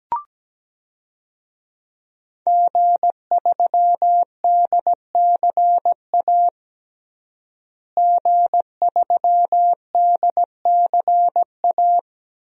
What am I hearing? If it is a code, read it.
Morse code, heard as a keyed tone, about 17 wpm, 700 Hz. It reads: G3DCA G3DCA
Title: DXCC-100 Call Signs - 5 Characters - Repeated 3x - 17wpm